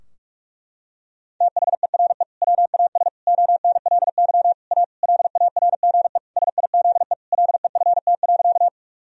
Morse code, heard as a keyed tone, about 46 wpm, 700 Hz. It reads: THERE WAS ONLY A LARGE SIZE LEFT.
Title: Sentences from Top 1000 Words - Review Part 2 - 45wpm